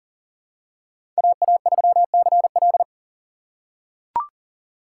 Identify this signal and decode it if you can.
Morse code, heard as a keyed tone, about 40 wpm, 700 Hz. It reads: AA3CL
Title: USA Call Signs - 6x Character Spacing - 40wpm